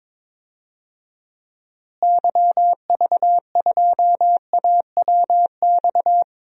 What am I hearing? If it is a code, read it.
Morse code, heard as a keyed tone, about 22 wpm, 700 Hz. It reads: YV2AWX
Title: DXCC-100 Call Signs - 6 Characters - Repeated 2x - 22wpm